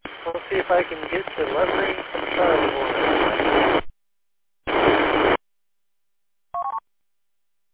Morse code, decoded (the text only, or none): none